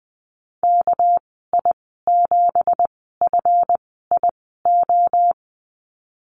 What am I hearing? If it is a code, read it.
Morse code, heard as a keyed tone, about 20 wpm, spaced as for 14 wpm, 700 Hz. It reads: KI7FIO
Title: 231 Lesson 37g - Letter x - Call Signs 20wpm